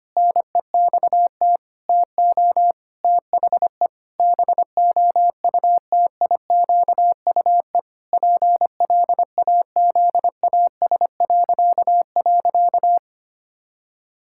Morse code, heard as a keyed tone, about 25 wpm, 700 Hz. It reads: NEXT TO THE BOUTIQUE PLAZAS..
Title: Pangram Challenge v2 - 25wpm